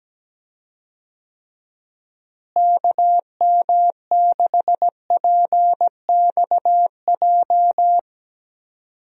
Morse code, heard as a keyed tone, about 17 wpm, 700 Hz. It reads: KM6PXJ
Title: USA Call Signs - 6 Characters - Repeated 3x - 17wpm